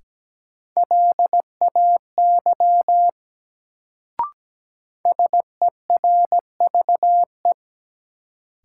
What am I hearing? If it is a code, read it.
Morse code, heard as a keyed tone, about 17 wpm, 700 Hz. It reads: LAY SERVE
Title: Sentences from Top 400 Words - New Word Drill - 17wpm